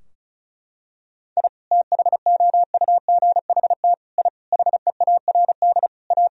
Morse code, heard as a keyed tone, about 34 wpm, 700 Hz. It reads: I THOUGHT I HEARD A
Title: Sentences from Top 1000 Words - Review Part 2 - 35wpm